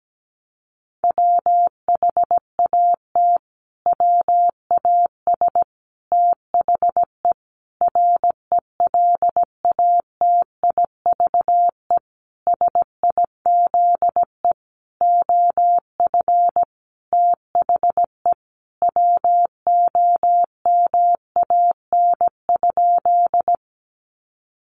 Morse code, heard as a keyed tone, about 17 wpm, 700 Hz. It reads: WHAT WAS THE RELATIVE SIZE OF THE WOMAN?
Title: Sentences from Top 1400 Words 17wpm